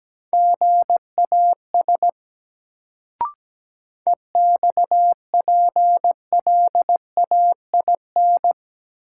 Morse code, heard as a keyed tone, about 17 wpm, 700 Hz. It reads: GAS EXPLAIN
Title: Sentences from Top 1100 Words - New Word Drill Alt - 17wpm